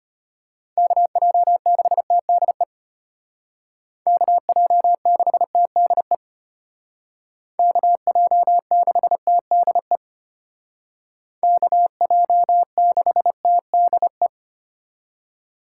Morse code, heard as a keyed tone, about 30 wpm, 700 Hz. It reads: KJ6TDE KJ6TDE KJ6TDE KJ6TDE
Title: Speed Racing - USA 2x3 Call Signs 25wpm